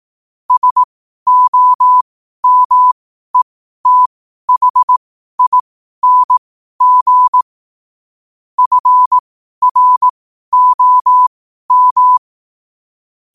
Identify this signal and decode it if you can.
Morse code, heard as a keyed tone, about 18 wpm, spaced as for 12 wpm, 1000 Hz. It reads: SOMETHING FROM
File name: qotdfiles / 12wpm